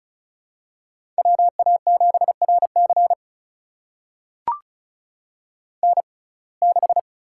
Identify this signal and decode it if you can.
Morse code, heard as a keyed tone, about 36 wpm, spaced as for 25 wpm, 700 Hz. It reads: WA7RC N6
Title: USA Call Signs - 5 Characters - 6x Character Spacing - 35wpm